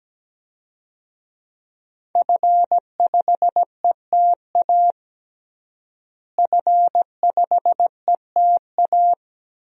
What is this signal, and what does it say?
Morse code, heard as a keyed tone, about 17 wpm, 700 Hz. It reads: F5ETA F5ETA
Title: DXCC-100 Call Signs - 5 Characters - Repeated 2x - 17wpm